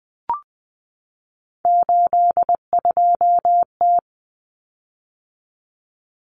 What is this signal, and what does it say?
Morse code, heard as a keyed tone, about 20 wpm, 700 Hz. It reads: 82T
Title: Common Cut Numbers - 3-Digit 20wpm